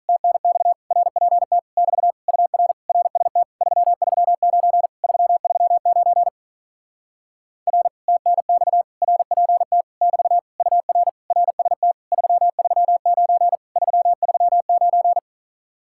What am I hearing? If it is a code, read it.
Morse code, heard as a keyed tone, about 46 wpm, 700 Hz. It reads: TNX RPT <BT> UR RST 339 339 R TNX RPT <BT> UR RST 339 339